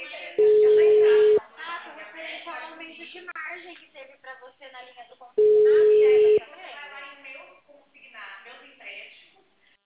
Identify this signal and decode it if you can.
Ringback tone (Europe, 425 Hz), with speech.